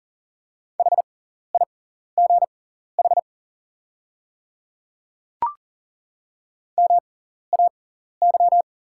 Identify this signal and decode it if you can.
Morse code, heard as a keyed tone, about 40 wpm, spaced as for 14 wpm, 690 Hz. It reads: HIGH MAY
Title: 6x Character Spacing - Top 200 Words - 40wpm